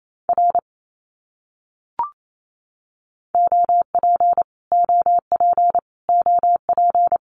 Morse code, heard as a keyed tone, about 28 wpm, 700 Hz. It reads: R OP OP OP
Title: Abbreviations Repeated 3x - 28wpm